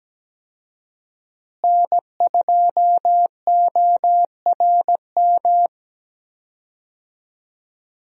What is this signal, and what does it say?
Morse code, heard as a keyed tone, about 17 wpm, 700 Hz. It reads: N2ORM
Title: USA Call Signs - 5 Characters - Repeated 3x - 17wpm